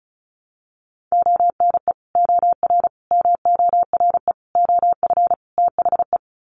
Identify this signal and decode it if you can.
Morse code, heard as a keyed tone, about 34 wpm, 700 Hz. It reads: ONE OR MORE OF THE
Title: Sets of 5 Words Encore 35wpm